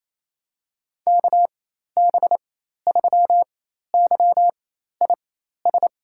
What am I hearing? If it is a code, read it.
Morse code, heard as a keyed tone, about 28 wpm, spaced as for 13 wpm, 700 Hz. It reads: KB3YIS
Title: USA Call Signs - 6 Characters - 4x Character Spacing - 28wpm